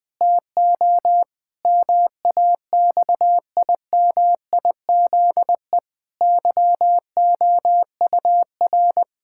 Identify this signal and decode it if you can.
Morse code, heard as a keyed tone, about 20 wpm, 700 Hz. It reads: TO MAXIMIZE YOUR
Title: Pangram Challenge v2 - 20wpm